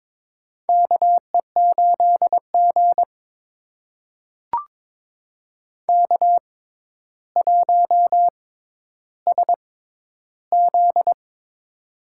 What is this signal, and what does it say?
Morse code, heard as a keyed tone, about 22 wpm, spaced as for 11 wpm, 700 Hz. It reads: KE8G K1SZ
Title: USA Call Signs - 4 Characters - 6x Character Spacing - 22wpm